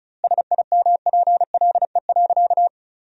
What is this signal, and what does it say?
Morse code, heard as a keyed tone, about 34 wpm, 700 Hz. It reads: SIMPLE.
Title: 2x Word-Spacing - Sentences from Top 500 Words - 35wpm